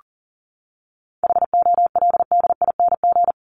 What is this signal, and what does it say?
Morse code, heard as a keyed tone, about 40 wpm, 700 Hz. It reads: HOLDING
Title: ING Suffix - Top 100 Words with Base Words 40wpm